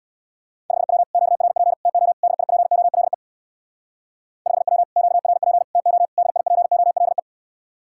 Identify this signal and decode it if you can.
Morse code, heard as a keyed tone, about 60 wpm, 700 Hz. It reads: 5W QRP TO DIPOLE 5W QRP TO DIPOLE
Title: Speed-Racing Common QSO Phrases - 50wpm